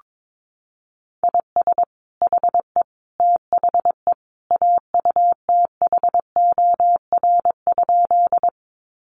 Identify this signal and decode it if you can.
Morse code, heard as a keyed tone, about 22 wpm, 700 Hz. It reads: IS HE THE AUTHOR?
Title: Sentences from Top 1200 Words 22wpm